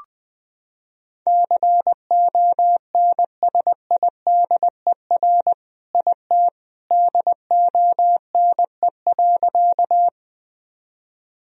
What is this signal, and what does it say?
Morse code, heard as a keyed tone, about 20 wpm, 700 Hz. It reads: CONSIDER IT DONE.
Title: Sentences from Top 1000 Words - Review Part 4 - 20wpm